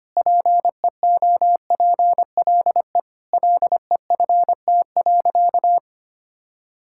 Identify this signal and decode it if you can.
Morse code, heard as a keyed tone, about 25 wpm, 700 Hz. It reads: PEOPLE LEFT.
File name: practice / 25wpm / Sentences with 4-Word Phrases - 25wpm